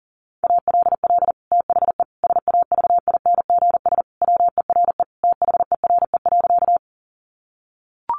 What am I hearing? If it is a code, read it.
Morse code, heard as a keyed tone, about 40 wpm, 700 Hz. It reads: ALL THE SAVINGS WERE THERE.
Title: Sentences from Top 1200 Words 40wpm